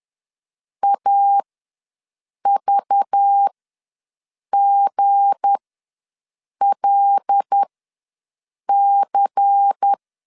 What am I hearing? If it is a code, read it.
Morse code, heard as a keyed tone, about 10 wpm, spaced as for 6 wpm, 800 Hz. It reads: AVGLC
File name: LEZIONI / 10 SIGNS / 10B INT ws 10wpm 4134